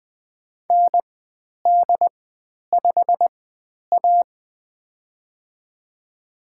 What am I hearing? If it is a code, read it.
Morse code, heard as a keyed tone, about 20 wpm, spaced as for 10 wpm, 700 Hz. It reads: ND5A
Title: 095 Lesson 16g - Letter c - Call Signs 20wpm